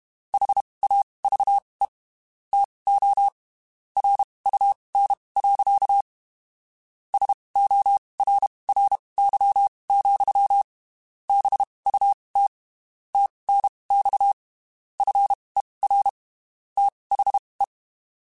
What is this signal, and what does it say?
Morse code, heard as a keyed tone, about 32 wpm, spaced as for 23 wpm, 800 Hz. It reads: HAVE TO RUN. SORRY, BUT TNX FER THE
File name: (spaced) / QSOs Spaced-mp3s / WA7QOX 2 @30 sp